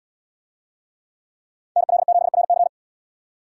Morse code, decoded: NL7RC

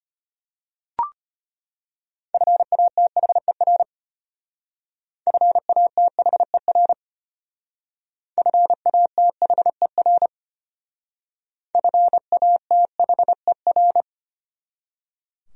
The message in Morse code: FATHER FATHER FATHER FATHER

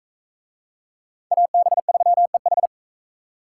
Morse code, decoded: AB3EH

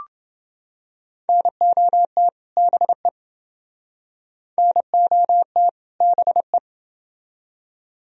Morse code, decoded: NOT BE NOT BE

